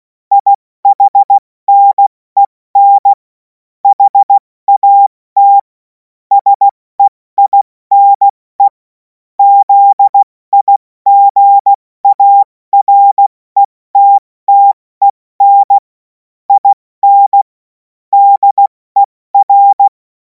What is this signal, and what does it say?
Morse code, heard as a keyed tone, about 16 wpm, 800 Hz. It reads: IHNEN HAT SEINE ZIGARETTEN IN DER
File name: Morsekurs / Witze / Witz 10. März 2026 70 F80